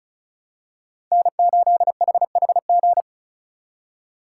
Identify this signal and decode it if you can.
Morse code, heard as a keyed tone, about 36 wpm, 700 Hz. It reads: N8HHG
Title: K1USN SST QSO Practice - 35wpm